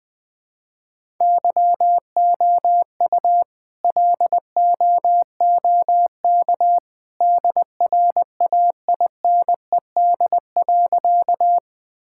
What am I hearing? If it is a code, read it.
Morse code, heard as a keyed tone, about 21 wpm, 700 Hz. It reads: YOU LOOK DRAINED.
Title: All-Age Appropriate Jokes - 20wpm